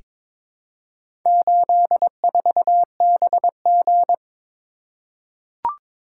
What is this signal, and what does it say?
Morse code, heard as a keyed tone, about 22 wpm, 700 Hz. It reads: 84BG
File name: practice / 22wpm / Four Letter-Number Combinations - 22wpm